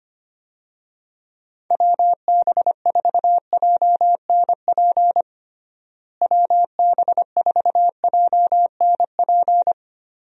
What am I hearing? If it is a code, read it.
Morse code, heard as a keyed tone, about 25 wpm, 700 Hz. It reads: WB4JNP WB4JNP
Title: USA Call Signs - 6 Characters - Repeated 3x - 25wpm